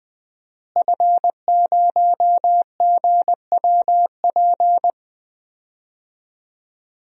Morse code, heard as a keyed tone, about 20 wpm, 700 Hz. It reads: F0GWP
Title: DXCC-100 Call Signs - 5 Characters - Repeated 2x - 20wpm